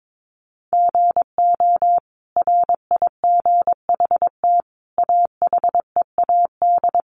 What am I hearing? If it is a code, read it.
Morse code, heard as a keyed tone, about 22 wpm, 700 Hz. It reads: GO RIGHT AHEAD
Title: Sentences from Top 1100 Words 22wpm